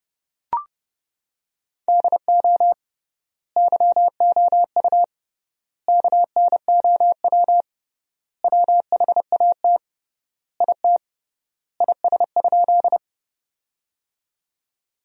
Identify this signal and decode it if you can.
Morse code, heard as a keyed tone, about 30 wpm, 700 Hz. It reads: DO YOU KNOW WHAT IT IS?
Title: 3x Word-Spacing - Sentences from Top 100 Words - 30wpm